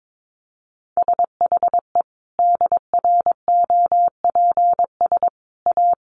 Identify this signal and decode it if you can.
Morse code, heard as a keyed tone, about 22 wpm, 700 Hz. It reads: SHE DROPS A
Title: Easy Phrases 22wpm